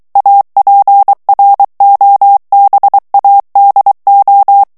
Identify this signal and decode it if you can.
Morse code, heard as a keyed tone, about 23 wpm, 800 Hz. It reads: APROBADO
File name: morse mp3 / ESPAGNOL TEXTE EN CLAIR 2 VITESSE 22 MOTS MINUTES